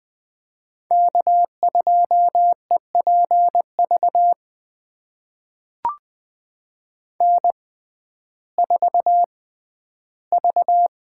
Morse code, heard as a keyed tone, about 20 wpm, spaced as for 12 wpm, 700 Hz. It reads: K2EPV N4V
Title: USA Call Signs - 5 Characters - 6x Character Spacing - 20wpm